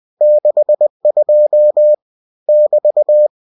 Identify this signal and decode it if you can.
Morse code, heard as a keyed tone, about 20 wpm, 600 Hz. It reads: 62 <BT>